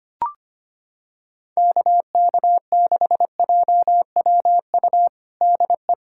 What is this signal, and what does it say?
Morse code, heard as a keyed tone, about 25 wpm, 700 Hz. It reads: KK6JWU DE
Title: USA Callsign DE Callsign 25wpm